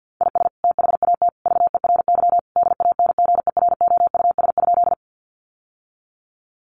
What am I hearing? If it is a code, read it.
Morse code, heard as a keyed tone, about 50 wpm, 700 Hz. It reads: IS THAT VERY DANGEROUS?
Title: Sentences from Top 1200 Words 50wpm